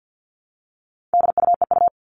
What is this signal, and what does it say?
Morse code, heard as a keyed tone, about 50 wpm, 700 Hz. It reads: DUEU
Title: All Cut Numbers - 4-Digit 50wpm